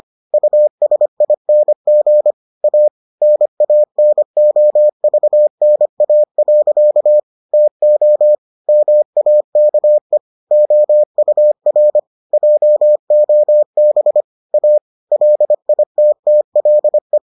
Morse code, heard as a keyed tone, about 25 wpm, 600 Hz. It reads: USING A NANOVNA. TO MAKE OUR JOB A LITTLE